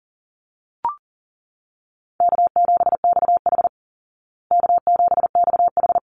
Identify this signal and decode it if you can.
Morse code, heard as a keyed tone, about 40 wpm, 700 Hz. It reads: K7XH K7XH
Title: Call Signs Repeated 3x - 40wpm